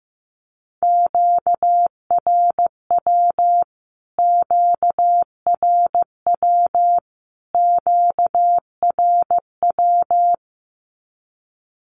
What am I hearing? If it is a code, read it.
Morse code, heard as a keyed tone, about 15 wpm, 700 Hz. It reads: QRW QRW QRW
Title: Q-Codes Repeated 3x 15wpm